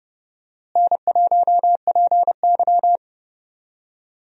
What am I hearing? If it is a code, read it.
Morse code, heard as a keyed tone, about 31 wpm, 700 Hz. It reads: N1PY
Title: USA Call Signs - 4 Characters - Repeated 2x - 30wpm